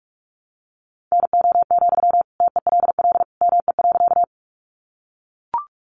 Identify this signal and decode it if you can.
Morse code, heard as a keyed tone, about 44 wpm, 700 Hz. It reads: NO, TELL ME.
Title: Sentences from Top 200 Words 45wpm